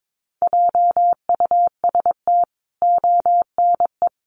Morse code, heard as a keyed tone, about 22 wpm, 700 Hz. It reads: JUST ONE
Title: Sentences from Top 1000 Words 22wpm